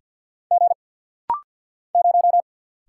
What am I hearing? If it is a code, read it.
Morse code, heard as a keyed tone, about 50 wpm, 700 Hz. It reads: G 0
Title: Single Letter-Number Alt - Rapid-Fire 50wpm